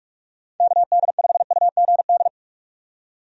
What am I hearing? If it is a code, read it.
Morse code, heard as a keyed tone, about 48 wpm, 700 Hz. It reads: KD5UGD
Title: USA Call Signs - 6 Characters - 45wpm